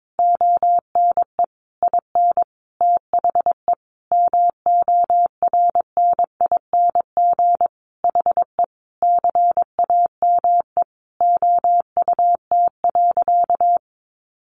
Morse code, heard as a keyed tone, about 22 wpm, 700 Hz. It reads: ONE IN THE MORNING HE CAME OUT.
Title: Sentences from Top 1300 Words 22wpm